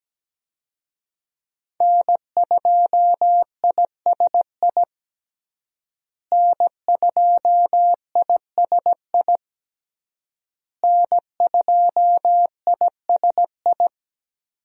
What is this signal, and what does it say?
Morse code, heard as a keyed tone, about 17 wpm, 700 Hz. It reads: N2ISI N2ISI N2ISI
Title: USA Call Signs - 5 Characters - Repeated 3x - 17wpm